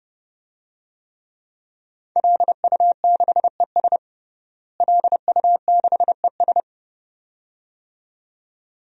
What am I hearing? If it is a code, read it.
Morse code, heard as a keyed tone, about 30 wpm, 700 Hz. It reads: LU6ES LU6ES
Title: DXCC-100 Call Signs - 5 Characters - Repeated 2x - 30wpm